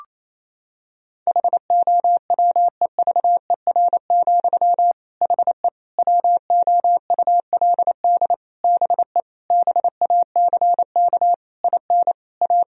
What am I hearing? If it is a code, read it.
Morse code, heard as a keyed tone, about 28 wpm, 700 Hz. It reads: HOWEVER, HE WOULD BE BACK IN A
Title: Sentences from Top 1000 Words - Review Part 2 - 28wpm